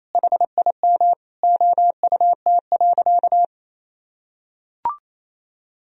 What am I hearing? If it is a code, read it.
Morse code, heard as a keyed tone, about 27 wpm, 700 Hz. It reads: HIM OUT.
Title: Sentences from Top 1000 Words - Review Part 3 - 28wpm